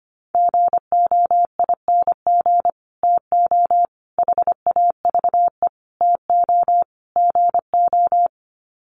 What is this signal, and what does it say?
Morse code, heard as a keyed tone, about 25 wpm, 700 Hz. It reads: GOING TO HAVE TO GO
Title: Sets of 5 Words Encore 25wpm